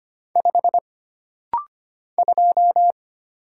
Morse code, heard as a keyed tone, about 25 wpm, 700 Hz. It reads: 5 2
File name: Single Character (Letter:Number:Punctuation:Prosign) - Rapid-Fire - 25wpm